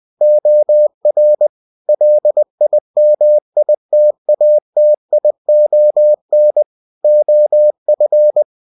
Morse code, heard as a keyed tone, about 20 wpm, 600 Hz. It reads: OR LIMITATION OF
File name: cw-books / winnie-the-pooh / winnie-20wpm-page-0119